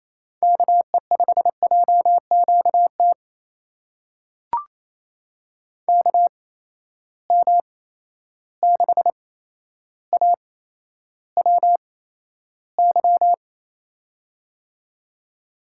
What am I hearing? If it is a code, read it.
Morse code, heard as a keyed tone, about 28 wpm, spaced as for 12 wpm, 700 Hz. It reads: KE5JQT KM6AWY
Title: USA Call Signs - 6 Characters - 8x Character Spacing - 28wpm